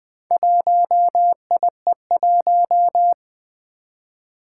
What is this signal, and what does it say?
Morse code, heard as a keyed tone, about 20 wpm, 700 Hz. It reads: 1IE1
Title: Four Challenging Letter-Number Combinations 20wpm